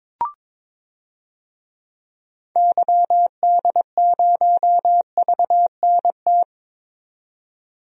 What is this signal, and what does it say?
Morse code, heard as a keyed tone, about 22 wpm, 700 Hz. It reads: YD0VNT
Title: DXCC-100 Call Signs - 6 Characters - 22wpm